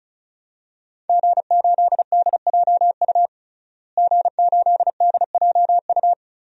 Morse code, heard as a keyed tone, about 36 wpm, 700 Hz. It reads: G8DJU G8DJU
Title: DXCC-100 Call Signs - 5 Characters - Repeated 3x - 35wpm